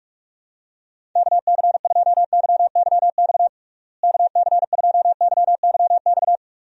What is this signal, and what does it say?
Morse code, heard as a keyed tone, about 46 wpm, 700 Hz. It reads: KC2YYX KC2YYX
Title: USA Call Signs - 6 Characters - Repeated 2x - 45wpm